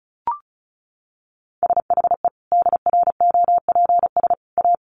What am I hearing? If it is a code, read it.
Morse code, heard as a keyed tone, about 34 wpm, 700 Hz. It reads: SHE DROPS A
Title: Additional Easy Phrases 35wpm